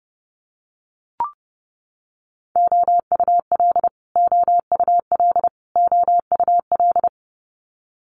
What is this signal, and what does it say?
Morse code, heard as a keyed tone, about 30 wpm, 700 Hz. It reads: OUL OUL OUL
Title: Multi-letter Phonemes Repeated 3x 30wpm